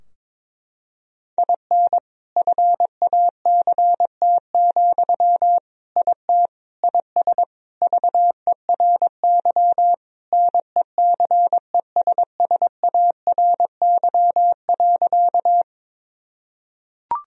Morse code, heard as a keyed tone, about 22 wpm, 700 Hz. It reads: IN FACT, IT IS VERY NECESSARY.
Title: Sentences from Top 1000 Words - Review Part 3 - 22wpm